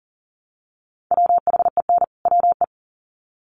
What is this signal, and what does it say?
Morse code, heard as a keyed tone, about 40 wpm, 700 Hz. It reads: WHEN WE
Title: Sets of 2 Words 40wpm